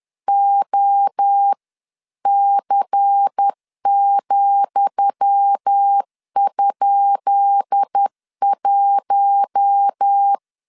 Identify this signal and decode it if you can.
Morse code, heard as a keyed tone, about 10 wpm, 800 Hz. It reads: OC,?1
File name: LEZIONI / 10 SIGNS / 10D COMA 10wpm 3013